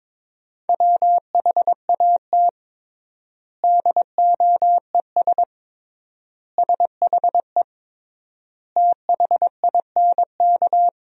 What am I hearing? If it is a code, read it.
Morse code, heard as a keyed tone, about 22 wpm, 700 Hz. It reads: WHAT DOES SHE THINK
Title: 3x Word-Spacing - Sentences from Top 200 Words - 22wpm